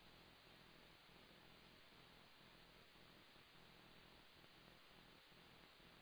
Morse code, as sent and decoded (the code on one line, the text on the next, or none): none